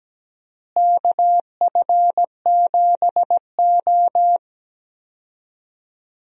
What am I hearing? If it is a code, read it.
Morse code, heard as a keyed tone, about 17 wpm, 700 Hz. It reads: KF7O
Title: USA Call Signs - 4 Characters - Repeated 3x - 17wpm